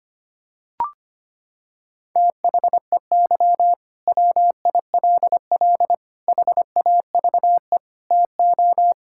Morse code, heard as a keyed tone, about 25 wpm, 700 Hz. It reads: THEY WILL HAVE TO
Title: Sentences from Top 1000 Words - Review Part 6 - 25wpm